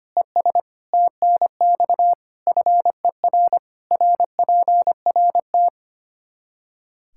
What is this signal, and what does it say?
Morse code, heard as a keyed tone, about 25 wpm, 700 Hz. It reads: ES TNX FER RPRT
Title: Practice QSO - Exchange 2 - Info - 25wpm